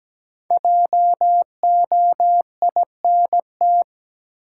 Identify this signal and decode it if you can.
Morse code, heard as a keyed tone, about 17 wpm, 700 Hz. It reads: JOINT
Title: Sentences from Top 1300 Words - New Word Drill Alt - 17wpm